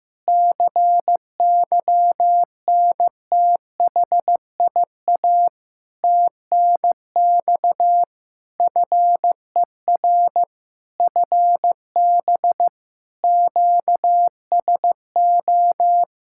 Morse code, heard as a keyed tone, about 15 wpm, 700 Hz. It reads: CYNTHIA TNX FER FB QSO
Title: K1USN SST QSO Practice 15wpm